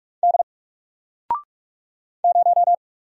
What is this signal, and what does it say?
Morse code, heard as a keyed tone, about 45 wpm, 700 Hz. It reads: D 0
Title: Single Letter-Number 45wpm